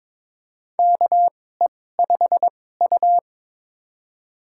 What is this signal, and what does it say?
Morse code, heard as a keyed tone, about 22 wpm, spaced as for 16 wpm, 700 Hz. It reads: KE5U
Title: USA Call Signs - 4 Characters - 2x Character Spacing - 22wpm